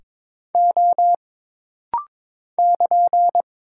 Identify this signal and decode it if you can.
Morse code, heard as a keyed tone, about 22 wpm, 700 Hz. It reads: O <KN>
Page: This rapid-fire timing is geared to push your ICR (Instant Character Recognition) skills.